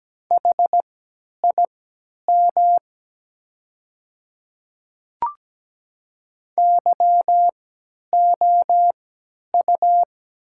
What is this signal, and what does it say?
Morse code, heard as a keyed tone, about 17 wpm, spaced as for 10 wpm, 700 Hz. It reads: HIM YOU
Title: 3x Character Spacing - Top 200 Words - 17wpm